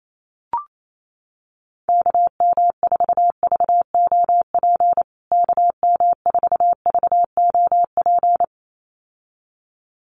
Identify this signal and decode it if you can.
Morse code, heard as a keyed tone, about 28 wpm, 700 Hz. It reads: KM4VOP KM4VOP
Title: USA Call Signs Repeated 2x 28wpm